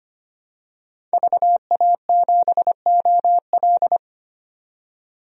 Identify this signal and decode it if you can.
Morse code, heard as a keyed tone, about 25 wpm, 700 Hz. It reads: VA7OL
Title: DXCC-100 Call Signs - 5 Characters - 25wpm